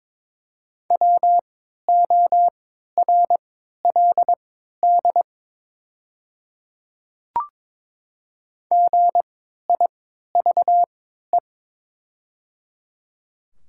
Morse code, heard as a keyed tone, about 22 wpm, spaced as for 13 wpm, 700 Hz. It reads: WORLD GIVE
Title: 3x Character Spacing - Top 200 Words - 22wpm